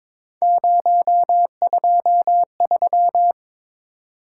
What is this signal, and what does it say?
Morse code, heard as a keyed tone, about 22 wpm, 700 Hz. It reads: 023
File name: practice / 22wpm / Three Letter-Number Combinations - 22wpm